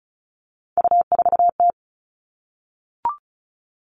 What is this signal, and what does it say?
Morse code, heard as a keyed tone, about 34 wpm, 700 Hz. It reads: U4T
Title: All Cut Numbers - 3-Digit 35wpm